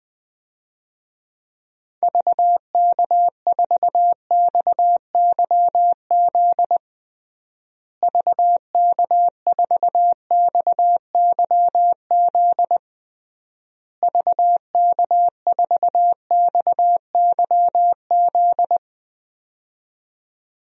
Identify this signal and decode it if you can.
Morse code, heard as a keyed tone, about 20 wpm, 700 Hz. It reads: VK4XYZ VK4XYZ VK4XYZ
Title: DXCC-100 Call Signs - 6 Characters - Repeated 3x - 20wpm